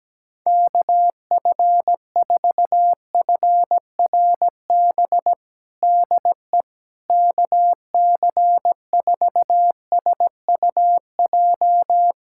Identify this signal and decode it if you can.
Morse code, heard as a keyed tone, about 17 wpm, 700 Hz. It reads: KF4FRB DE KC4SUJ
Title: USA Callsign DE Callsign - 6 Characters - 17wpm